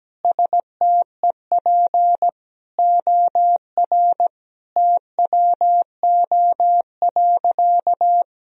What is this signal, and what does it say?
Morse code, heard as a keyed tone, about 17 wpm, 700 Hz. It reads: STEP OR TWO.
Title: Sentences from Top 1000 Words - Review Part 2 - 17wpm